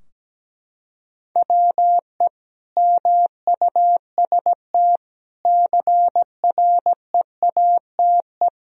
Morse code, heard as a keyed tone, about 17 wpm, 700 Hz. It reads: WE MUST CREATE
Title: Sentences from Top 1000 Words - Review Part 3 - 17wpm